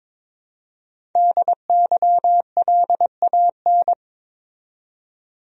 Morse code, heard as a keyed tone, about 22 wpm, 700 Hz. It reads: DYLAN
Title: USA Common Names - Max 5 Letters - 22wpm